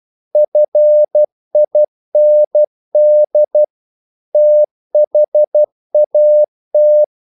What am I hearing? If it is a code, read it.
Morse code, heard as a keyed tone, about 12 wpm, 600 Hz. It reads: FIND THAT
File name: CW / Practice / EnglishQuotations0112